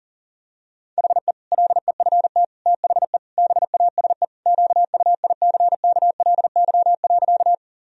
Morse code, heard as a keyed tone, about 40 wpm, 700 Hz. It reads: HE LEFT THE BASE QUICKLY.
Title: Sentences from Top 1000 Words - Review Part 1 - 40wpm